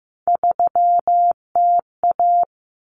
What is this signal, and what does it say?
Morse code, heard as a keyed tone, about 15 wpm, 700 Hz. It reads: 3TA
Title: Common Cut Numbers - 3-Digit 15wpm